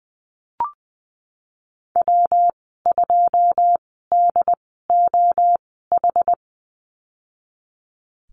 Morse code, heard as a keyed tone, about 20 wpm, spaced as for 15 wpm, 700 Hz. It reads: W2DOH
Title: 213 Lesson 34g - Number 8 - Call Signs 20wpm